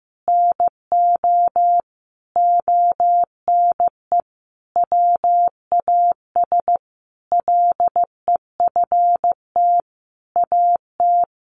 Morse code, heard as a keyed tone, about 15 wpm, 700 Hz. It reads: NO ONE WAS LEFT AT
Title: Sentences from Top 1400 Words 15wpm